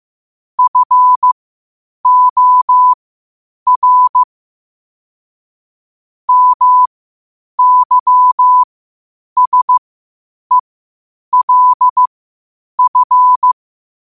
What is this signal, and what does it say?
Morse code, heard as a keyed tone, about 15 wpm, spaced as for 8 wpm, 1000 Hz. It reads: FOR MYSELF
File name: 8wpm